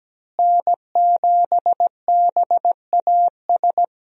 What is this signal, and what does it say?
Morse code, heard as a keyed tone, about 17 wpm, 700 Hz. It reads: N7BAS
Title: USA Call Signs - 5 Characters - Repeated 2x - 17wpm